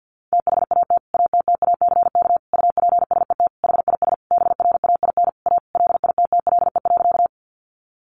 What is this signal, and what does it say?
Morse code, heard as a keyed tone, about 50 wpm, 700 Hz. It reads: THAT ATTACK UPSET HIS BRAIN A LITTLE.
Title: Sentences from Top 1500 Words 50wpm